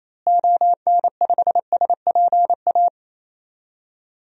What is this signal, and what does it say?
Morse code, heard as a keyed tone, about 28 wpm, 700 Hz. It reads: ON5SPA
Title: DXCC-100 Call Signs - 6 Characters - Repeated 2x - 28wpm